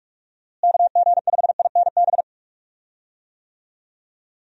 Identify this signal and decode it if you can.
Morse code, heard as a keyed tone, about 43 wpm, 700 Hz. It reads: KG5INB
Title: USA Call Signs - 6 Characters - Repeated 2x - 45wpm